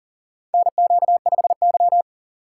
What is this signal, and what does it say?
Morse code, heard as a keyed tone, about 38 wpm, 690 Hz. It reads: NQ5Y
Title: USA Call Signs - 4 Characters - 2x Character Spacing - 40wpm